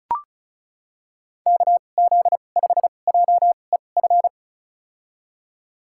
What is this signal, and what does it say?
Morse code, heard as a keyed tone, about 36 wpm, spaced as for 25 wpm, 700 Hz. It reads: KZ5JEF
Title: USA Call Signs - 6 Characters - 2x Character Spacing - 35wpm